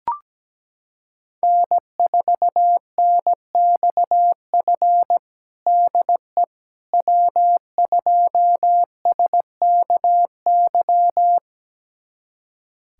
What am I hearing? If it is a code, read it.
Morse code, heard as a keyed tone, about 17 wpm, 700 Hz. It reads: N4NXF DE W2SKY
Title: USA Callsign DE Callsign - 5 Characters - 17wpm